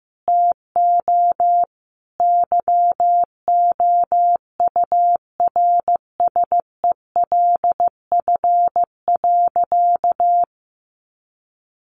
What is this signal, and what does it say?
Morse code, heard as a keyed tone, about 15 wpm, 700 Hz. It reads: TO YOURSELF.
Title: Sentences from Top 1100 Words 15wpm